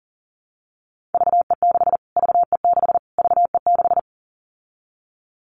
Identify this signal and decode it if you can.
Morse code, heard as a keyed tone, about 40 wpm, 700 Hz. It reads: VE6 VE6 VE6
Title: Callsign Prefixes for Top 25 DXCC - Repeated 3x - 40wpm